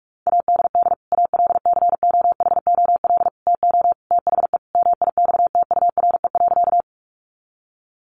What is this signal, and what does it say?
Morse code, heard as a keyed tone, about 44 wpm, 700 Hz. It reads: ADD ALCOHOL TO THE MIXTURE.
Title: Sentences from Top 1500 Words 45wpm